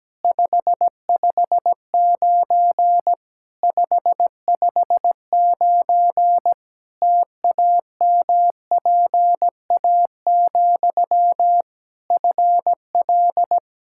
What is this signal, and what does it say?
Morse code, heard as a keyed tone, about 17 wpm, 700 Hz. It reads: 559 559 TAMPA, FL
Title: Practice QSO - Exchange 1 - Beginning - 17wpm